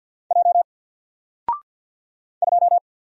Morse code, heard as a keyed tone, about 48 wpm, 700 Hz. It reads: J 2
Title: Single Character (Letter:Number:Punctuation:Prosign) Alt 50wpm